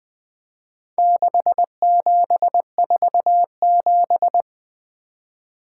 Morse code, heard as a keyed tone, about 24 wpm, 700 Hz. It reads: 6747
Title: Speed-Racing - Four Letter-Number Combinations - 20wpm